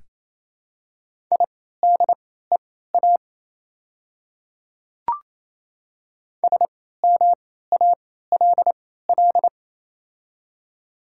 Morse code, heard as a keyed tone, about 28 wpm, spaced as for 16 wpm, 700 Hz. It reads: IDEA SMALL
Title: 3x Character Spacing - Top 500 Words - 28wpm